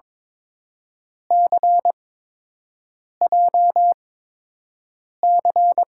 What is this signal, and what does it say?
Morse code, heard as a keyed tone, about 22 wpm, 700 Hz. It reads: C J C
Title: USA Call Signs - 5 Characters - 8x Character Spacing - 22wpm